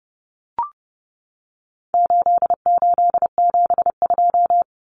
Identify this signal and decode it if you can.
Morse code, heard as a keyed tone, about 30 wpm, 700 Hz. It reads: 8872
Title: Numbers - 4-Digit 30wpm